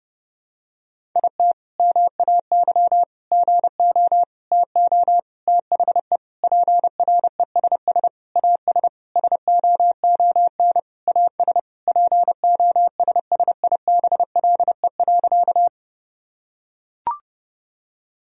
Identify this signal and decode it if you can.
Morse code, heard as a keyed tone, about 30 wpm, 700 Hz. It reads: IT MAY GO TO THE PRESS AS SOON AS POSSIBLE.
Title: Sentences from Top 1000 Words - Review Part 1 - 30wpm